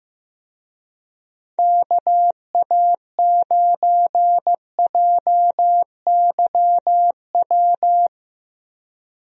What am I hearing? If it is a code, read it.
Morse code, heard as a keyed tone, about 15 wpm, 700 Hz. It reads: KA9JYW
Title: USA Call Signs - 6 Characters - 15wpm